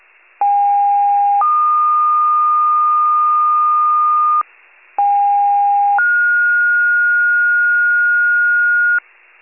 Two-tone page, heard as a one-second tone, then a three-second tone, about 810 then 1220 Hz, about 810 then 1430 Hz, with noise. It is heard from 0.4 to 9.0 s.